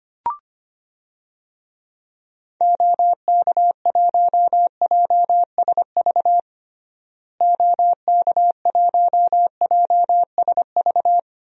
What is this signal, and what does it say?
Morse code, heard as a keyed tone, about 25 wpm, 700 Hz. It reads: OK1JSV OK1JSV
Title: DXCC-100 Call Signs - 6 Characters - Repeated 2x - 25wpm